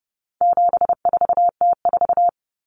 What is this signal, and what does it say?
Morse code, heard as a keyed tone, about 30 wpm, 700 Hz. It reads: 74T4
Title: Common Cut Numbers - 4-Digit 30wpm